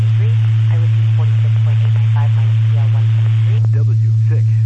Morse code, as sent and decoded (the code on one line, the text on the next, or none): none